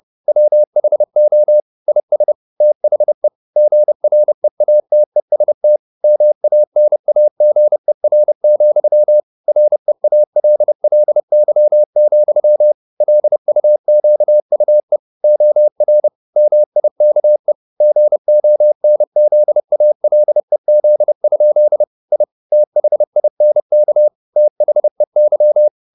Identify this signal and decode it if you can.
Morse code, heard as a keyed tone, about 30 wpm, 600 Hz. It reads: WHO IS THE GREATEST MANAGER, REALLY, LUQUE OR MIKE GONZALEZ? I THINK THEY
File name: the old man and the sea / the old man and the sea 1 30wpm